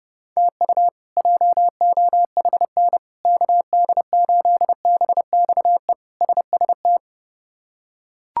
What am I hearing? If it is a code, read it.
Morse code, heard as a keyed tone, about 30 wpm, 700 Hz. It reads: TU JOHN KD8BXE SST
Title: K1USN SST QSO Practice - 30wpm